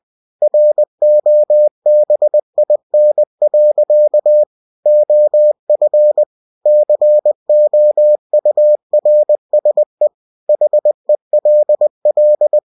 Morse code, heard as a keyed tone, about 20 wpm, 600 Hz. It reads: ROBIN. OF COURSE HELL
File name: cw-books / winnie-the-pooh / winnie-20wpm-page-0018